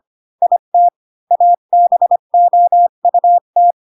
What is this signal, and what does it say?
Morse code, heard as a keyed tone, about 25 wpm, 700 Hz. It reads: IT ABOUT
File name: Pinocchio / Pinocchio ch1 20 25 ik1hge